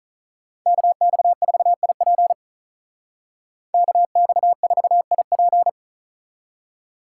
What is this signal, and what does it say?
Morse code, heard as a keyed tone, about 43 wpm, 700 Hz. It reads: KX4IP KX4IP
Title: Speed Racing - USA 2x2 Call Signs 35wpm